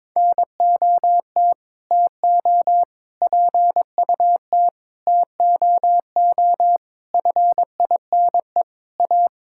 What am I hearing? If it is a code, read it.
Morse code, heard as a keyed tone, about 21 wpm, 700 Hz. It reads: NOT TO PUT TOO FINE A
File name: Sets of 9 Words - 22wpm